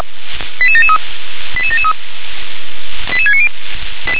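Five-tone selective call (ZVEI), heard as quick runs of five tones, tones of about 70 ms. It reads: 88773 88773 88690
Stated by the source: ZVEI five-tone selective call received on an SDR.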